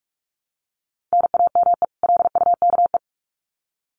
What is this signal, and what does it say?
Morse code, heard as a keyed tone, about 44 wpm, 700 Hz. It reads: NAME LUKE
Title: Common QSO Phrases 45wpm